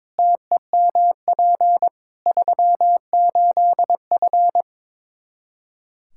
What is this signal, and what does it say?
Morse code, heard as a keyed tone, about 22 wpm, 700 Hz. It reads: TEMP 38F
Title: Practice QSO - Exchange 2 - Info - 22wpm